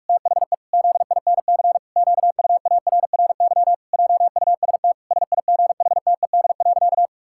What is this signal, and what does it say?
Morse code, heard as a keyed tone, about 46 wpm, 700 Hz. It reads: THE ZINC QUARRY JUST SIGHTED.
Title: Pangram Challenge v2 - 45wpm